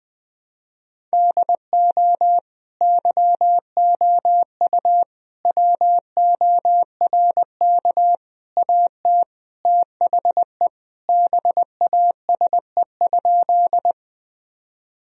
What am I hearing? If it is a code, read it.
Morse code, heard as a keyed tone, about 21 wpm, 700 Hz. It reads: DO YOU WORK AT THE BASE?
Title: Sentences from Top 1000 Words - Review Part 1 - 20wpm